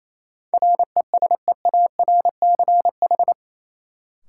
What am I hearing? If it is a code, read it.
Morse code, heard as a keyed tone, about 28 wpm, 700 Hz. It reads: RESEARCH
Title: Sentences from Top 1000 Words - Review Part 4 - New Word Drill - 28wpm